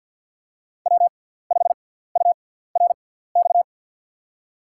Morse code, heard as a keyed tone, about 47 wpm, spaced as for 17 wpm, 700 Hz. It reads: W5URX
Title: USA Call Signs - 5 Characters - 6x Character Spacing - 50wpm